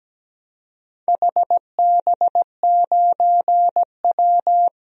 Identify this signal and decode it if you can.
Morse code, heard as a keyed tone, about 17 wpm, 700 Hz. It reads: HB9W
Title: DXCC-100 Call Signs - 4 Characters - Repeated 2x - 17wpm